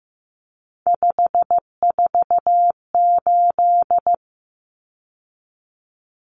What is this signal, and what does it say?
Morse code, heard as a keyed tone, about 15 wpm, 700 Hz. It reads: 548
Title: Numbers - 3-Digit 15wpm